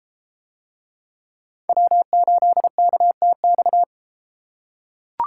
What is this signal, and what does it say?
Morse code, heard as a keyed tone, about 30 wpm, 700 Hz. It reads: W8KTX